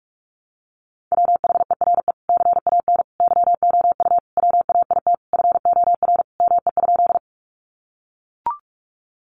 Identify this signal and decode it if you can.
Morse code, heard as a keyed tone, about 44 wpm, 700 Hz. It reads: WHERE CAN YOU WAIT FOR ME?
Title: Sentences from Top 500 Words 45wpm